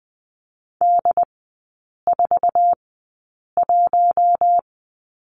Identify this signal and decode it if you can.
Morse code, heard as a keyed tone, about 20 wpm, spaced as for 8 wpm, 700 Hz. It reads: D41
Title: 107 Lesson 18e - Letter m - Three Characters 20wpm